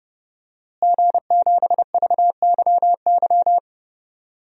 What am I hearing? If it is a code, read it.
Morse code, heard as a keyed tone, about 30 wpm, 700 Hz. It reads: G7VYY
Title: DXCC-100 Call Signs - 5 Characters - Repeated 2x - 30wpm